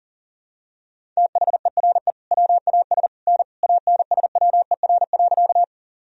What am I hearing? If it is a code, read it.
Morse code, heard as a keyed tone, about 40 wpm, 700 Hz. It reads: THERE WAS N ANSWER.
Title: Sentences from Top 300 Words and Abbreviations 40wpm